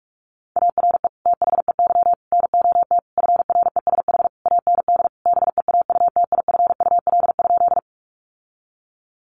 Morse code, heard as a keyed tone, about 44 wpm, 700 Hz. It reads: ARE THEY NOT FRESH AND BEAUTIFUL?
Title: Sentences from Top 1200 Words 45wpm